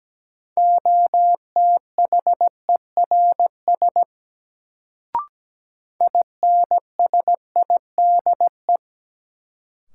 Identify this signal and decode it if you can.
Morse code, heard as a keyed tone, about 17 wpm, 700 Hz. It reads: OTHERS INSIDE
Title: Sentences from Top 1000 Words - Review Part 2 - New Word Drill - 17wpm